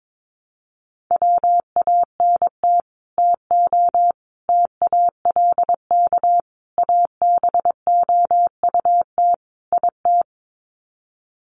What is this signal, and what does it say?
Morse code, heard as a keyed tone, about 22 wpm, 700 Hz. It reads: WANT TO TALK ABOUT IT
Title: Sets of 5 Words Encore 22wpm